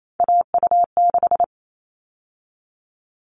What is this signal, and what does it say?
Morse code, heard as a keyed tone, about 28 wpm, 700 Hz. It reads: AU6
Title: All Cut Numbers - 3-Digit 28wpm